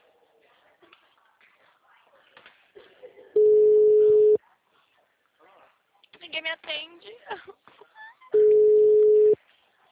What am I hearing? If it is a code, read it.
Ringback tone (Europe, 425 Hz), with speech.
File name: out- -2016- - - .32307